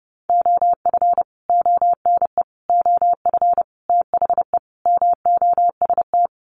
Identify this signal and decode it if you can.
Morse code, heard as a keyed tone, about 30 wpm, 700 Hz. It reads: OF ONE OF THE MOST
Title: Sets of 5 Words Encore2 30wpm